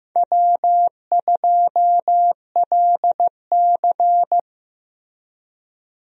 Morse code, heard as a keyed tone, about 15 wpm, 700 Hz. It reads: W2LC
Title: USA Call Signs - 4 Characters - Repeated 2x - 15wpm